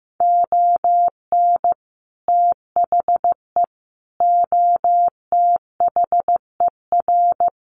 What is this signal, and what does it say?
Morse code, heard as a keyed tone, about 15 wpm, 700 Hz. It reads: ON THE OTHER
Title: Sets of 5 Words Encore 15wpm